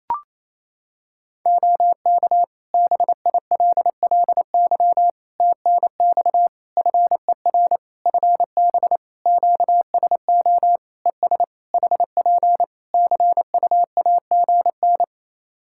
Morse code, heard as a keyed tone, about 28 wpm, 700 Hz. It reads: OK BILLY TNX FER FB QSO ES HP CUAGN
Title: Practice QSO - Exchange 4 - Ending - 28wpm